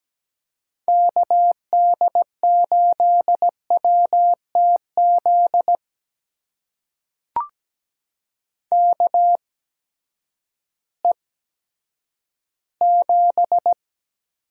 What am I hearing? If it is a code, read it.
Morse code, heard as a keyed tone, about 17 wpm, spaced as for 10 wpm, 700 Hz. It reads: KD8WTZ KE7
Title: USA Call Signs - 8x Character Spacing - 17wpm